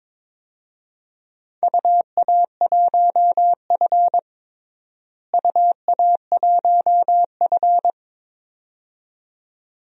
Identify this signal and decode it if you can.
Morse code, heard as a keyed tone, about 22 wpm, 700 Hz. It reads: UA1F UA1F
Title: DXCC-100 Call Signs - 4 Characters - Repeated 2x - 22wpm